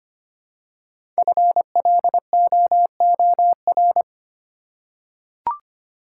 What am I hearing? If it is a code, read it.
Morse code, heard as a keyed tone, about 25 wpm, 700 Hz. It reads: FLOOR
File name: 5 Letter Words - First 3 Letters Given - 25wpm